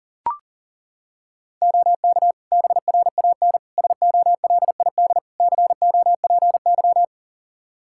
Morse code, heard as a keyed tone, about 40 wpm, 700 Hz. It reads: OK BRAN SOLID COPY
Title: Practice QSO - Exchange 3 - More Info - 40wpm